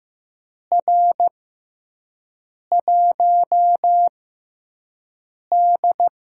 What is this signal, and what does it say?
Morse code, heard as a keyed tone, about 15 wpm, spaced as for 5 wpm, 700 Hz. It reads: R1D
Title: USA Call Signs - 4 Characters - 6x Character Spacing - 15wpm